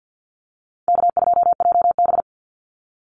Morse code, heard as a keyed tone, about 50 wpm, 700 Hz. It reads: K2JB
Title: NA SOTA Chaser Callsigns 50wpm